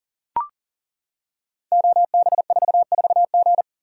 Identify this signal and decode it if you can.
Morse code, heard as a keyed tone, about 38 wpm, 690 Hz. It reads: OB44G
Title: Code Groups - 5 Letter:Numbers - 40wpm